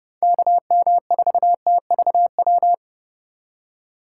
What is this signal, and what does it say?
Morse code, heard as a keyed tone, about 29 wpm, 700 Hz. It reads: KM4TVW